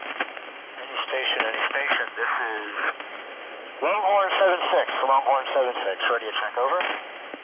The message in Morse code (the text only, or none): none